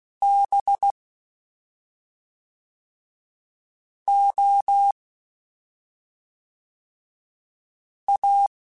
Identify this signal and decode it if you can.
Morse code, heard as a keyed tone, about 16 wpm, 800 Hz. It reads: B O A
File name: Special INGs / Special INGs-mp3s / Fast ing @10